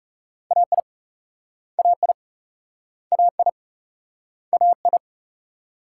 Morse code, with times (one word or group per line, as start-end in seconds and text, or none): 0.5-0.8 AI
1.8-2.1 AI
3.1-3.5 AI
4.5-5.0 AI